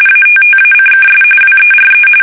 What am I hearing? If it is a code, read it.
RTTY, about 45.45 baud. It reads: ATTACKS WERE
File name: fsk850 / battleshipnj